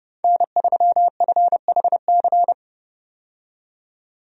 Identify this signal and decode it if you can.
Morse code, heard as a keyed tone, about 30 wpm, 700 Hz. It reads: N3FHC
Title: USA Call Signs - 5 Characters - 30wpm